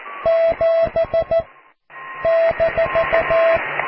Morse code, heard as a keyed tone, about 14 wpm, 660 Hz, with radio noise. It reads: 7 -
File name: cw / QSO+QRM